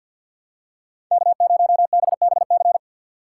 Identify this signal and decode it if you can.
Morse code, heard as a keyed tone, about 52 wpm, 700 Hz. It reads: K0BBC
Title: USA Call Signs - 5 Characters - 4x Character Spacing - 50wpm